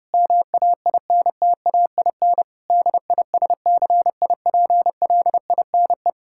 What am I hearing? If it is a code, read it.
Morse code, heard as a keyed tone, about 32 wpm, 700 Hz. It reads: MAINTAIN DISCIPLINE
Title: Sentences from Top 1000 Words - Review Part 6 - 30wpm